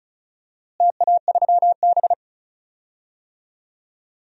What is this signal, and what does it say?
Morse code, heard as a keyed tone, about 36 wpm, 700 Hz. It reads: TA3B
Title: DXCC-100 Call Signs - 4 Characters - Repeated 3x - 35wpm